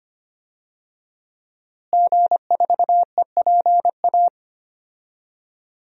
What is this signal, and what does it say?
Morse code, heard as a keyed tone, about 25 wpm, 700 Hz. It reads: G4EPA
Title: DXCC-100 Call Signs - 5 Characters - Repeated 2x - 25wpm